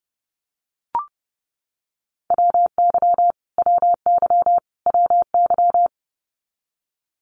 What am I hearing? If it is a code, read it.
Morse code, heard as a keyed tone, about 30 wpm, 700 Hz. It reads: WY WY WY
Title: US States Repeated 3x 30wpm